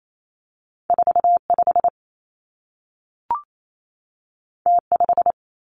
Morse code, heard as a keyed tone, about 28 wpm, 700 Hz. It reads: 4 5 T 5
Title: Two Challenging Letter-Number Combinations 28wpm